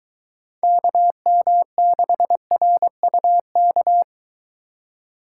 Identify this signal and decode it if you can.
Morse code, heard as a keyed tone, about 23 wpm, 700 Hz. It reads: KM6RUK